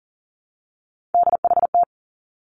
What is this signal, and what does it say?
Morse code, heard as a keyed tone, about 40 wpm, 700 Hz. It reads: DHT
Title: Three Challenging Letter-Number Combinations 40wpm